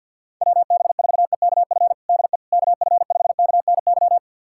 Morse code, heard as a keyed tone, about 47 wpm, 700 Hz. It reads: WB4EKF DE KF5KNY
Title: USA Callsign DE Callsign - 6 Characters - 50wpm